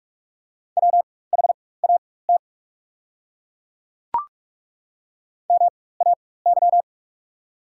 Morse code, heard as a keyed tone, about 43 wpm, spaced as for 21 wpm, 690 Hz. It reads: WHAT MAY
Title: 4x Character Spacing - Top 200 Words - 45wpm